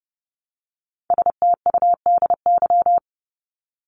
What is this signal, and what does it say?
Morse code, heard as a keyed tone, about 30 wpm, 700 Hz. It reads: STUDY
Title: ES:S Suffix - Top 100 Words with Base Words 30wpm